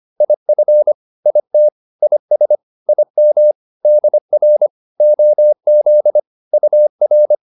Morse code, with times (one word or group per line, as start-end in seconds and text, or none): 0.2-0.9 IF
1.3-1.7 IT
2.0-2.6 IS
2.9-3.5 IM
3.9-4.7 DR
5.0-6.2 OZ
6.5-7.3 UR